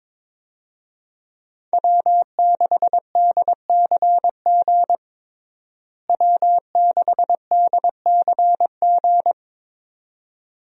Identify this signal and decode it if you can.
Morse code, heard as a keyed tone, about 22 wpm, 700 Hz. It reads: W6DCG W6DCG
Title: USA Call Signs - 5 Characters - Repeated 2x - 22wpm